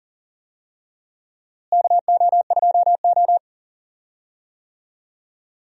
Morse code, heard as a keyed tone, about 40 wpm, 700 Hz. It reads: KO2O
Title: USA Call Signs - 4 Characters - Repeated 2x - 40wpm